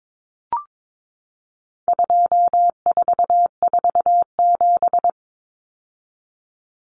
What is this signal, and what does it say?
Morse code, heard as a keyed tone, about 22 wpm, 700 Hz. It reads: 2447
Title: Numbers - 4-Digit 22wpm